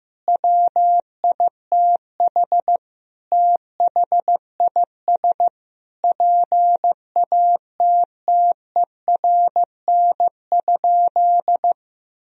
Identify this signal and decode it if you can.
Morse code, heard as a keyed tone, about 15 wpm, 700 Hz. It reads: WITH THIS PATTERN?
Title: Sentences from Top 1000 Words - Review Part 2 - 15wpm